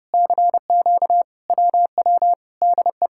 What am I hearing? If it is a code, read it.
Morse code, heard as a keyed tone, about 29 wpm, 700 Hz. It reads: CQ WW DE